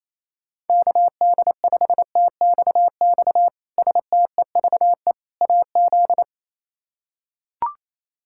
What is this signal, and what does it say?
Morse code, heard as a keyed tone, about 27 wpm, 700 Hz. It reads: KD5TXX STEVE AZ
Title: K1USN SST QSO Practice - 28wpm